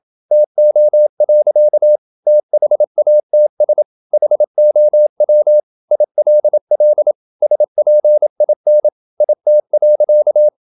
Morse code, heard as a keyed tone, about 27 wpm, 600 Hz. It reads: TO. THATS HOW ILL SPIN IT.
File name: Never 0027wpm